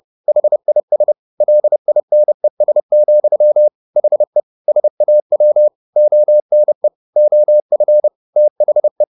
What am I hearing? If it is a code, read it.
Morse code, heard as a keyed tone, about 30 wpm, 600 Hz. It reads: HIS LINES, HE SAW ONE OF THE
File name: the old man and the sea / the old man and the sea 1 30wpm